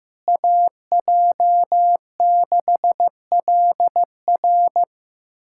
Morse code, heard as a keyed tone, about 15 wpm, 700 Hz. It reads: AJ6LR
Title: USA Callsign DE Callsign - 5 Characters - 15wpm